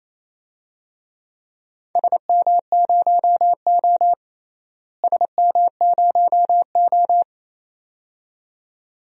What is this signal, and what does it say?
Morse code, heard as a keyed tone, about 28 wpm, 700 Hz. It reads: SM0O SM0O
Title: DXCC-100 Call Signs - 4 Characters - Repeated 2x - 28wpm